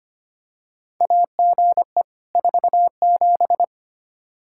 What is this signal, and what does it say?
Morse code, heard as a keyed tone, about 25 wpm, 700 Hz. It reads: AGE 47
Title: Practice QSO - Exchange 3 - More Info - 25wpm